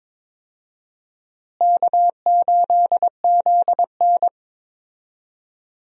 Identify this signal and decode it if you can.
Morse code, heard as a keyed tone, about 22 wpm, 700 Hz. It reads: K8ZN
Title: USA Call Signs - 4 Characters - Repeated 3x - 22wpm